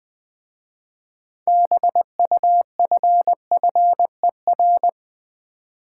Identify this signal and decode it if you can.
Morse code, heard as a keyed tone, about 20 wpm, 700 Hz. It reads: BUFFER
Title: 6 Letter Words - First 3 Letters Given - 20wpm